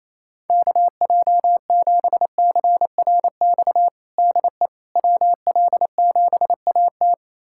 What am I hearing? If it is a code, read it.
Morse code, heard as a keyed tone, about 28 wpm, 700 Hz. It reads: KJ7CRX DE WL7AT
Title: USA Callsign DE Callsign 28wpm